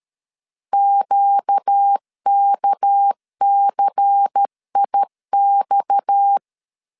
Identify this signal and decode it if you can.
Morse code, heard as a keyed tone, about 13 wpm, 800 Hz. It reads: QKCIX
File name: LEZIONI / 6 PKX0 / 6 PKX 12wpm 2024